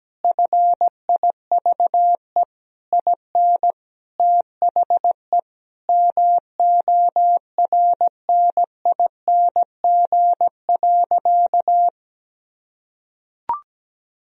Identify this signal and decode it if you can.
Morse code, heard as a keyed tone, about 17 wpm, 700 Hz. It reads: FIVE IN THE MORNING.
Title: Sentences from Top 1000 Words - Review Part 2 - 17wpm